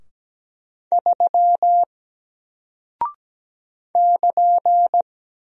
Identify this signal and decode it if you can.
Morse code, heard as a keyed tone, about 17 wpm, 700 Hz. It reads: 3 <KN>
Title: Single Character (Letter:Number:Punctuation:Prosign) Alt 17wpm